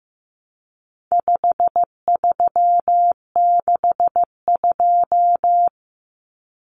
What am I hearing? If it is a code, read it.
Morse code, heard as a keyed tone, about 15 wpm, 700 Hz. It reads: 5362
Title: Common Cut Numbers - 4-Digit 15wpm